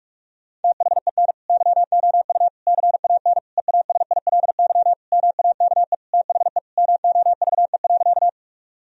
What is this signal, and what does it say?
Morse code, heard as a keyed tone, about 41 wpm, 700 Hz. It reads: THEN YOU CAN EASILY MAKE THE MOVE.
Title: Sentences from Top 1000 Words - Review Part 1 - 45wpm